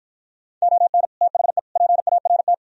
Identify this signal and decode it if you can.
Morse code, heard as a keyed tone, about 53 wpm, 700 Hz. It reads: ON THE PART